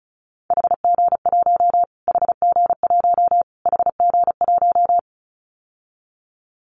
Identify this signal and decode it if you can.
Morse code, heard as a keyed tone, about 34 wpm, 700 Hz. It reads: HG1 HG1 HG1
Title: Callsign Prefixes for Top 25 DXCC - Repeated 3x - 35wpm